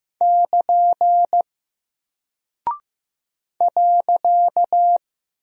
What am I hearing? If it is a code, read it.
Morse code, heard as a keyed tone, about 15 wpm, 700 Hz. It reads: <KN> .
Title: Single Character (Letter:Number:Punctuation:Prosign) Alt 15wpm